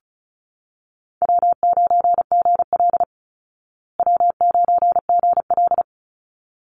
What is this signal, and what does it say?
Morse code, heard as a keyed tone, about 34 wpm, 700 Hz. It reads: W9GL W9GL
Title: Call Signs Repeated 3x - 35wpm